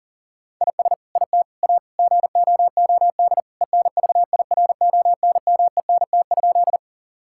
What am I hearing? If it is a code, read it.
Morse code, heard as a keyed tone, about 40 wpm, 700 Hz. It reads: IS IT A GOOD ENVIRONMENT?
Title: Sentences from Top 1000 Words - Review Part 5 - 40wpm